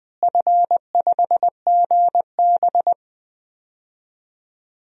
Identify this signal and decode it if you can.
Morse code, heard as a keyed tone, about 19 wpm, 690 Hz. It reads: F5GB
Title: DXCC-100 Call Signs - 4 Characters - Repeated 3x - 20wpm